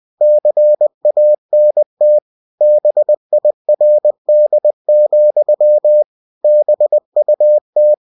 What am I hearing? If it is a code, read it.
Morse code, heard as a keyed tone, about 20 wpm, 600 Hz. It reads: CANT BIRD, BUT
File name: cw-books / winnie-the-pooh / winnie-20wpm-page-0045